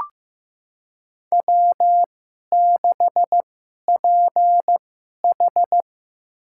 Morse code, heard as a keyed tone, about 15 wpm, spaced as for 11 wpm, 700 Hz. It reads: W6PH
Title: USA Call Signs - 4 Characters - 2x Character Spacing - 15wpm